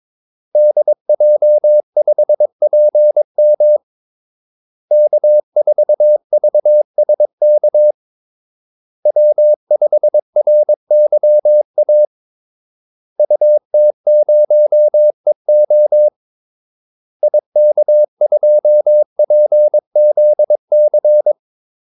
Morse code, heard as a keyed tone, about 22 wpm, 600 Hz. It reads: DJ5PM K4VSK W5RYA UT0EO IK2PZC
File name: Callsigns-22wpm